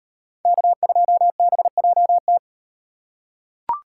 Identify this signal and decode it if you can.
Morse code, heard as a keyed tone, about 38 wpm, 700 Hz. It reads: K2BJT